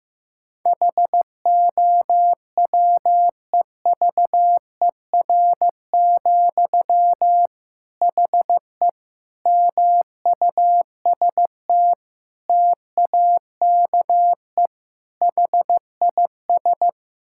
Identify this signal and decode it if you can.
Morse code, heard as a keyed tone, about 15 wpm, 700 Hz. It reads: HOWEVER, HE MUST TAKE HIS
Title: Sentences from Top 1600 Words 15wpm